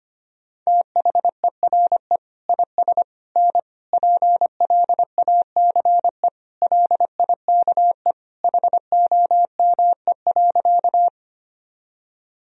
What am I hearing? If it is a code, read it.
Morse code, heard as a keyed tone, about 25 wpm, 700 Hz. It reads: THERE IS N PLACE LIKE HOME.
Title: Sentences from Top 200 Words and Abbreviations 25wpm